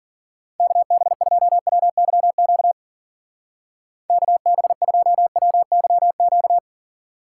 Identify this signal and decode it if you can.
Morse code, heard as a keyed tone, about 52 wpm, 700 Hz. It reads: KB2WYQ KB2WYQ